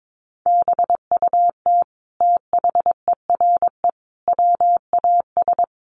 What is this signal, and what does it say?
Morse code, heard as a keyed tone, about 22 wpm, 700 Hz. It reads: BUT THERE WAS
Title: Sentences from Top 1400 Words 22wpm